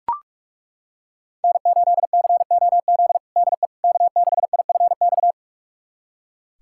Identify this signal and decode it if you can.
Morse code, heard as a keyed tone, about 41 wpm, 700 Hz. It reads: N8COZ DE K6IFX
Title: USA Callsign DE Callsign - 5 Characters - 45wpm